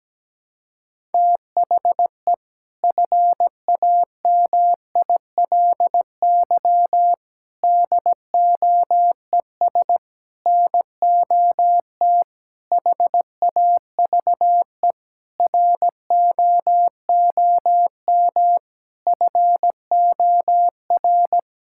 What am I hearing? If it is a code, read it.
Morse code, heard as a keyed tone, about 17 wpm, 700 Hz. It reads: THE FAMILY DOES NOT HAVE ROOM FOR
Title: Sentences from Top 1000 Words - Review Part 5 - 17wpm